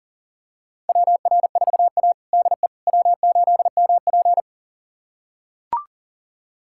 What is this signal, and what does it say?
Morse code, heard as a keyed tone, about 40 wpm, 700 Hz. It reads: WR4A DE W8MP
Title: USA Callsign DE Callsign - 4 Characters - 40wpm